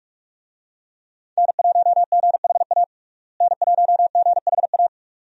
Morse code, heard as a keyed tone, about 42 wpm, 700 Hz. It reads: N1GHA N1GHA
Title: USA Call Signs - 5 Characters - Repeated 2x - 45wpm